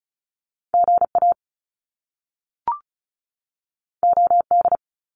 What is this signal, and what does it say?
Morse code, heard as a keyed tone, about 36 wpm, 700 Hz. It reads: GA OD
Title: Top Two Letter Combinations in Words 35wpm